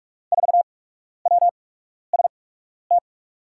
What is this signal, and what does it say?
Morse code, heard as a keyed tone, about 43 wpm, spaced as for 12 wpm, 700 Hz. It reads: 4WST
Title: USA Call Signs - 6 Characters - 8x Character Spacing - 45wpm